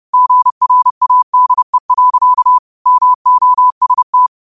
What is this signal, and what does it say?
Morse code, heard as a keyed tone, about 30 wpm, 1000 Hz. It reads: GRADE. MOST
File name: qotdfiles / 30wpm